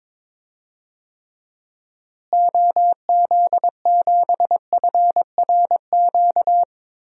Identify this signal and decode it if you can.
Morse code, heard as a keyed tone, about 21 wpm, 700 Hz. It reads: OZ7FRQ
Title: DXCC-100 Call Signs - 6 Characters - Repeated 3x - 22wpm